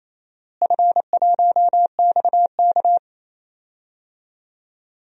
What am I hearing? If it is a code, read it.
Morse code, heard as a keyed tone, about 28 wpm, 700 Hz. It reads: F1XK
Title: DXCC-100 Call Signs - 4 Characters - Repeated 2x - 28wpm